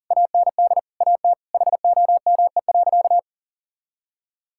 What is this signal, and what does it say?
Morse code, heard as a keyed tone, about 37 wpm, 700 Hz. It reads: AND AT HOME.
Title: Sentences from Top 1000 Words - Review Part 6 - 40wpm